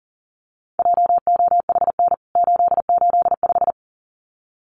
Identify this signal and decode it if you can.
Morse code, heard as a keyed tone, about 37 wpm, 700 Hz. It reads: JOHN 885
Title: CWT Exchange 40wpm